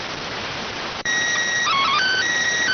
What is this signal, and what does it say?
SSTV (slow-scan television), PD 290.